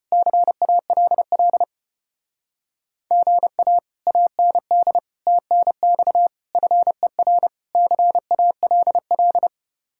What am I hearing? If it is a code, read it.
Morse code, heard as a keyed tone, about 33 wpm, 700 Hz. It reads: CALL GA AND TNX FER CALL